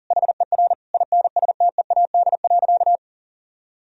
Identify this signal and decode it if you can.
Morse code, heard as a keyed tone, about 40 wpm, 700 Hz. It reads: HER INSTEAD.
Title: Sentences from Top 1000 Words - Review Part 4 - 40wpm